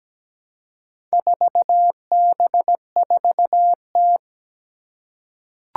Morse code, heard as a keyed tone, about 17 wpm, 700 Hz. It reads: 4B4T
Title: Four Challenging Letter-Number Combinations 17wpm